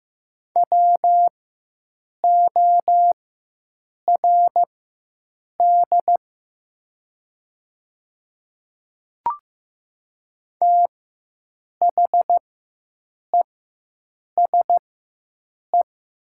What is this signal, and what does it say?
Morse code, heard as a keyed tone, about 15 wpm, spaced as for 7 wpm, 700 Hz. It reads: WORD THESE
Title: 4x Character Spacing - Top 500 Words - 15wpm